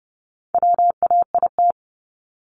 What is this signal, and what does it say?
Morse code, heard as a keyed tone, about 30 wpm, 700 Hz. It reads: WAIT
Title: ING Suffix - Top 100 Words with Base Words 30wpm